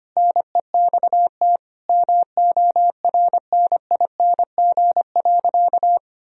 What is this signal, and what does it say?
Morse code, heard as a keyed tone, about 25 wpm, 700 Hz. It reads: NEXT MORNING.
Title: Sentences from Top 1000 Words - Review Part 2 - 25wpm